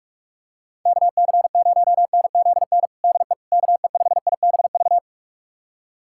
Morse code, heard as a keyed tone, about 48 wpm, 700 Hz. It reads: KC0NZN DE KE5IBV
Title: USA Callsign DE Callsign 45wpm